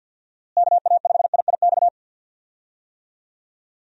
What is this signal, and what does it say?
Morse code, heard as a keyed tone, about 49 wpm, 700 Hz. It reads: KA5IIX
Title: USA Call Signs - 6 Characters - Repeated 3x - 50wpm